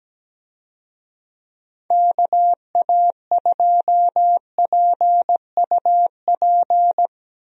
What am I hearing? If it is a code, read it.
Morse code, heard as a keyed tone, about 17 wpm, 700 Hz. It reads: KA2PUP
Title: USA Call Signs - 6 Characters - Repeated 3x - 17wpm